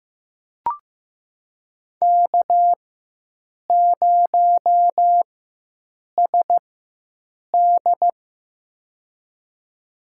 Morse code, heard as a keyed tone, about 15 wpm, spaced as for 7 wpm, 700 Hz. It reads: K0SD
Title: USA Call Signs - 4x Character Spacing - 15wpm